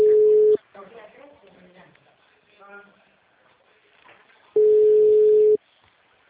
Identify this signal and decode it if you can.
Ringback tone (Europe, 425 Hz), with speech.